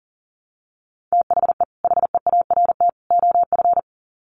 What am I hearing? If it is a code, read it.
Morse code, heard as a keyed tone, about 40 wpm, 700 Hz. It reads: THE HEART OF
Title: Sets of 3 Words 40wpm